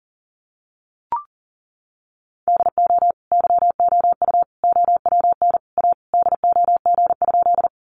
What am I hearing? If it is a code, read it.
Morse code, heard as a keyed tone, about 40 wpm, 700 Hz. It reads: DO YOU OWN A DOG?
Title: Sentences from Top 400 Words 40wpm